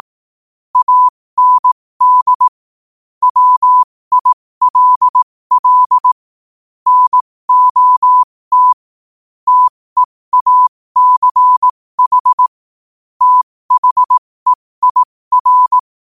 Morse code, heard as a keyed tone, about 18 wpm, spaced as for 15 wpm, 1000 Hz. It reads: AND WILL NOT TEACH THEIR
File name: qotdfiles / 15wpm